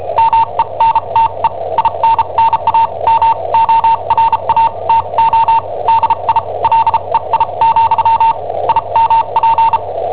Morse code, heard as a keyed tone, about 34 wpm, spaced as for 28 wpm, 910 Hz. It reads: MENTE INNAMORATO DI LEI, IMP
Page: Cassetta esercizi preparata da I7OHP per IW5ELP